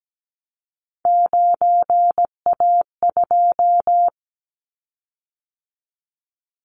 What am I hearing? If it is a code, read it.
Morse code, heard as a keyed tone, about 17 wpm, 700 Hz. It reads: 9A2
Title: Callsign Prefixes for Top 50 DXCC - 17wpm